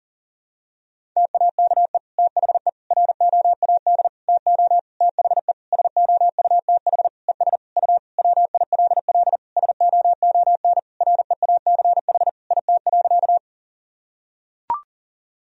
Morse code, heard as a keyed tone, about 40 wpm, 700 Hz. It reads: TAKE THE ROAD TO THE SOUTH ES U WILL SOON REACH IT.
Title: Sentences from Top 400 Words and Abbreviations 40wpm